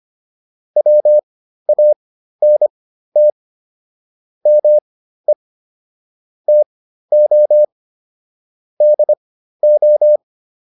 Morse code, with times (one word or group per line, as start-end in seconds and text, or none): 0.8-3.3 WANT
4.5-5.3 ME
6.5-7.6 TO
8.8-10.2 DO